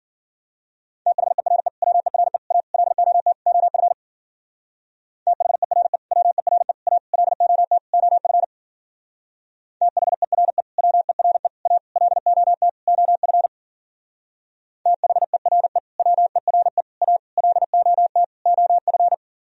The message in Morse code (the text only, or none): THERE WERE A LOT OF THERE WERE A LOT OF THERE WERE A LOT OF THERE WERE A LOT OF